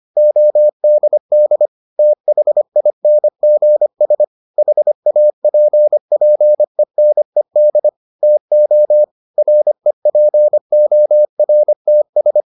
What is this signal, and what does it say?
Morse code, heard as a keyed tone, about 25 wpm, 600 Hz. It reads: ODD THINGS HAPPENED TO REPORTS